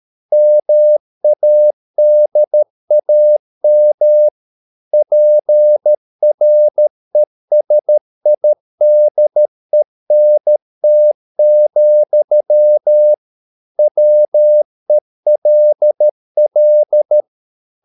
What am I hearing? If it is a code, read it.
Morse code, heard as a keyed tone, about 13 wpm, 600 Hz. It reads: MADAM PRESIDENT, WELL